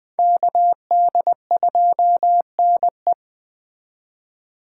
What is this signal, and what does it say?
Morse code, heard as a keyed tone, about 21 wpm, 700 Hz. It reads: KD2NE
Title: Speed Racing - USA 2x2 Call Signs 17wpm